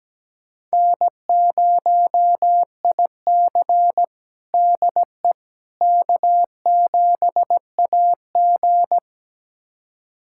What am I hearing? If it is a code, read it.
Morse code, heard as a keyed tone, about 17 wpm, 700 Hz. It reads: N0IC DE K7AG
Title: USA Callsign DE Callsign - 4 Characters - 17wpm